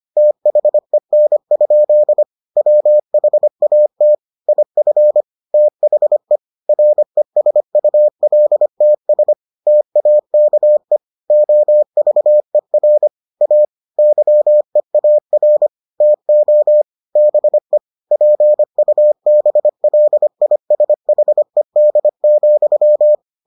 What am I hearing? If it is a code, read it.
Morse code, heard as a keyed tone, about 25 wpm, 600 Hz. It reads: THEN? WHAT IF THE RESULTS TAKE OVER A YEAR TO BE PUBLISHED,